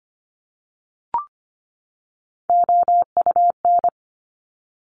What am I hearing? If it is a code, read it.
Morse code, heard as a keyed tone, about 25 wpm, 700 Hz. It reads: OUN
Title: Top Three Letter Combinations in Words 25wpm